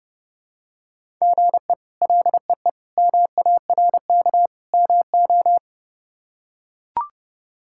Morse code, heard as a keyed tone, about 30 wpm, 700 Hz. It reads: GE LEE MARK MO
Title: K1USN SST QSO Practice - 30wpm